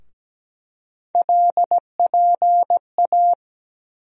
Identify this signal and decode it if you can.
Morse code, heard as a keyed tone, about 17 wpm, 700 Hz. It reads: LPA
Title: USA Call Sign Postfixes - 3 Letters 17wpm